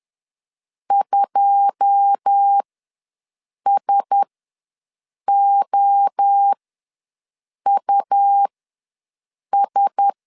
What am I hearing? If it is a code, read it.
Morse code, heard as a keyed tone, about 10 wpm, spaced as for 6 wpm, 800 Hz. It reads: 2SOUS
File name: LEZIONI / 9 28 ws 10wpm 4124